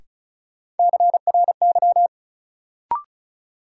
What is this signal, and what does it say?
Morse code, heard as a keyed tone, about 36 wpm, 700 Hz. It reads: CRY
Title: Sets of 5 Words Encore - New Word Drill - 35wpm